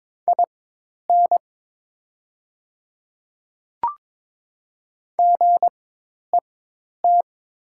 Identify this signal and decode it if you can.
Morse code, heard as a keyed tone, about 22 wpm, spaced as for 10 wpm, 700 Hz. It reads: IN GET
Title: 4x Character Spacing - Top 200 Words - 22wpm